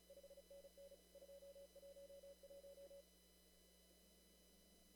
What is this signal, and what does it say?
Morse code, heard as a keyed tone, about 34 wpm, 560 Hz, with noise. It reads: 5NN 211
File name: CW / kt5j wpx cw-op-k5pi